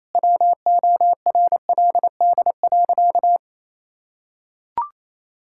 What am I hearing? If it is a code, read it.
Morse code, heard as a keyed tone, about 28 wpm, 700 Hz. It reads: WORLD.
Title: Sentences from Top 1000 Words - Review Part 6 - 28wpm